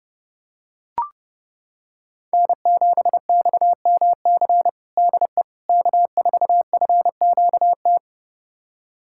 Morse code, heard as a keyed tone, about 32 wpm, 700 Hz. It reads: N7XMC DE K4FQT
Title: USA Callsign DE Callsign 30wpm